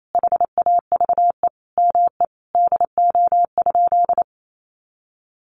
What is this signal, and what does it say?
Morse code, heard as a keyed tone, about 28 wpm, 700 Hz. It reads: HAVE ME DO?
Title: Sentences from Top 200 Words 28wpm